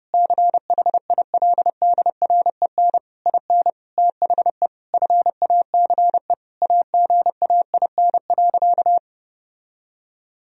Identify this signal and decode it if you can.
Morse code, heard as a keyed tone, about 30 wpm, 700 Hz. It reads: CHILDREN IN THE FACE AGAIN.
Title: Sentences from Top 1000 Words - Review Part 2 - 30wpm